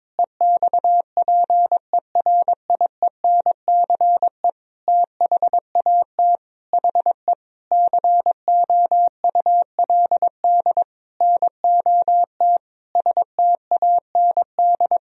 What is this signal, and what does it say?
Morse code, heard as a keyed tone, about 22 wpm, 700 Hz. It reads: EXPERIENCE THAT HE COULD NOT STAND
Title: Sentences from Top 1000 Words - Review Part 3 - 22wpm